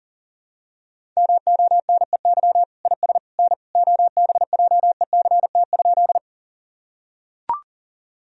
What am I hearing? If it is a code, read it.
Morse code, heard as a keyed tone, about 37 wpm, 700 Hz. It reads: MONEY IS N OBJECT?
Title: Sentences from Top 500 Words and Abbreviations 40wpm